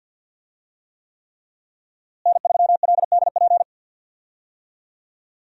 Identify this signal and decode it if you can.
Morse code, heard as a keyed tone, about 48 wpm, 700 Hz. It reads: N3LDP
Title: USA Call Signs - 5 Characters - Repeated 3x - 50wpm